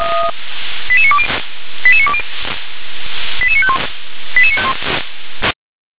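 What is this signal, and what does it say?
Five-tone selective call (ZVEI), heard as quick runs of five tones, tones of about 70 ms. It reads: 88920 88920 88951
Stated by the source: ZVEI five-tone selective call 88988, received via SDR.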